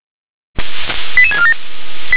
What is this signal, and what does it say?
Five-tone selective call (ZVEI), heard as quick runs of five tones, tones of about 70 ms. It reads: 88647